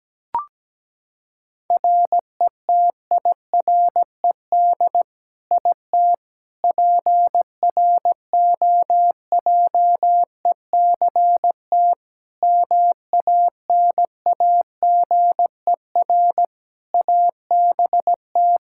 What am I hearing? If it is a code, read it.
Morse code, heard as a keyed tone, about 17 wpm, 700 Hz. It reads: RETIRED IT PROJECT MANAGER ABT
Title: Practice QSO - Exchange 3 - More Info - 17wpm